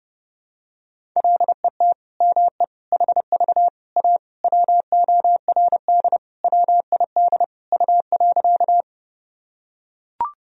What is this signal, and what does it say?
Morse code, heard as a keyed tone, about 29 wpm, 700 Hz. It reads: LET ME HV A WORD WID U.
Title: Sentences from Top 300 Words and Abbreviations 30wpm